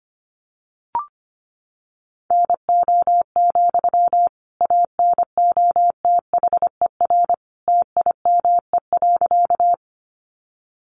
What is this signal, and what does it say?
Morse code, heard as a keyed tone, about 25 wpm, 700 Hz. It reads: NO, ANOTHER TIME.
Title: Sentences from Top 600 Words 25wpm